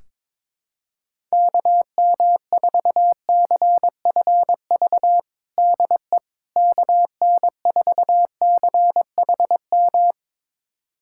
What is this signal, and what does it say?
Morse code, heard as a keyed tone, about 22 wpm, 700 Hz. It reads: KM4CFV DE KN4CHM
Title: USA Callsign DE Callsign - 6 Characters - 22wpm